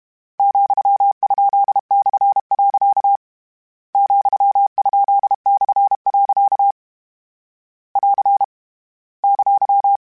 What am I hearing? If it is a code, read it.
Morse code, heard as a keyed tone, about 32 wpm, 800 Hz. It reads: ,?/. ,?/. <AR> !
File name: Misc / Misc-mp3s / Punct @30